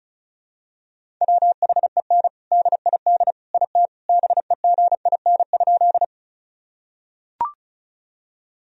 Morse code, heard as a keyed tone, about 34 wpm, 700 Hz. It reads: WHEN DID IT BEGIN?
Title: Sentences from Top 1000 Words - Review Part 1 - 35wpm